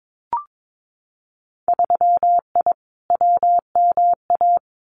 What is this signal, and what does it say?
Morse code, heard as a keyed tone, about 22 wpm, 700 Hz. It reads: 3I WMA
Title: Winter Field Day Exchanges 22wpm